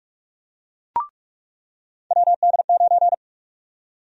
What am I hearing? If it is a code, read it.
Morse code, heard as a keyed tone, about 44 wpm, 700 Hz. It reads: WD9
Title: USA Call Sign Prefixes - 45wpm